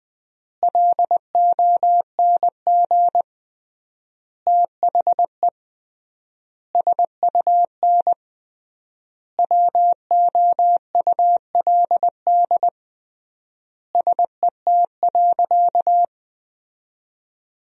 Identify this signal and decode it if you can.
Morse code, heard as a keyed tone, about 21 wpm, 700 Hz. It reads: LONG THE SUN WOULD SET.
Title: 3x Word-Spacing - Sentences from Top 400 Words - 20wpm